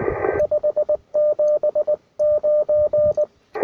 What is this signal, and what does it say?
Morse code, heard as a keyed tone, about 19 wpm, 600 Hz, with radio noise. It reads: 579
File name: attachments / UR4MCK IZ5GVP